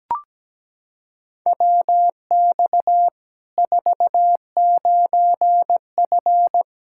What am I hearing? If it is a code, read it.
Morse code, heard as a keyed tone, about 17 wpm, 700 Hz. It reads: WX 49F
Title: Practice QSO - Exchange 2 - Info - 17wpm